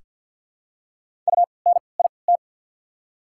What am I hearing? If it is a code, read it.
Morse code, heard as a keyed tone, about 48 wpm, spaced as for 28 wpm, 690 Hz. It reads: UNIT
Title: 3x Character Spacing - Top 500 Words - 50wpm